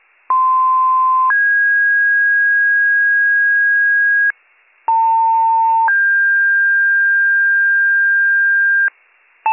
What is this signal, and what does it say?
Two-tone page, heard as a one-second tone, then a three-second tone, about 1040 then 1650 Hz, about 900 then 1600 Hz. It is heard from 0.3 to 8.9 s.